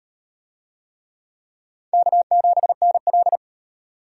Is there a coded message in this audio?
Yes: Morse code from 1.9 to 3.4 s, heard as a keyed tone, about 35 wpm, 700 Hz. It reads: K7NL